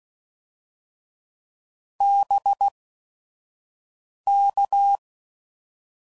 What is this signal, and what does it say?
Morse code, heard as a keyed tone, about 16 wpm, spaced as for 5 wpm, 800 Hz. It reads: BK
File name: CW Code Groups @5WPM / QSOs / WD0FBT 1